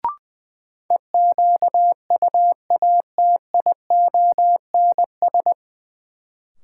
Morse code, heard as a keyed tone, about 20 wpm, 700 Hz. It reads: EQUATIONS
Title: Sets of 5 Words - New Word Drill - 20wpm